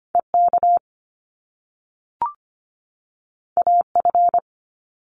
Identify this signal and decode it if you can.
Morse code, heard as a keyed tone, about 25 wpm, 700 Hz. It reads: EK AF
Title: Top Two Letter Combinations in Words 25wpm